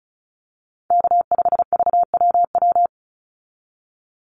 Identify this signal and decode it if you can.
Morse code, heard as a keyed tone, about 34 wpm, 700 Hz. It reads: K5VWW
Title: Call Signs - 35wpm